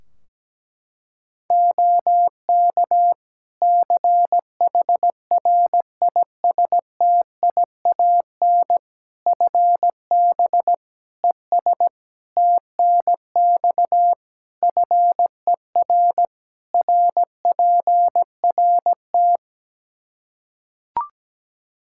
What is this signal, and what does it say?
Morse code, heard as a keyed tone, about 17 wpm, 700 Hz. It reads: OK CHRISTIAN FB ES TNX FER RPRT
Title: Practice QSO - Exchange 2 - Info - 17wpm